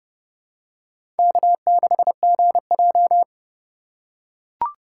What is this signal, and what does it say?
Morse code, heard as a keyed tone, about 29 wpm, 700 Hz. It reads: K6GJ
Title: USA Call Signs - 2x Character Spacing - 30wpm